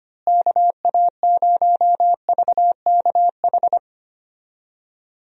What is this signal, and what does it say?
Morse code, heard as a keyed tone, about 25 wpm, 700 Hz. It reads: KA0VKH
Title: USA Call Signs - 6 Characters - Repeated 3x - 25wpm